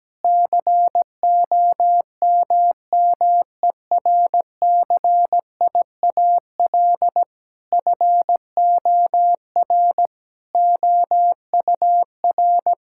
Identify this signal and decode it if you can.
Morse code, heard as a keyed tone, about 17 wpm, 700 Hz. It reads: COMMERCIAL FOR OUR
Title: Sentences from Top 1000 Words - Review Part 5 - 17wpm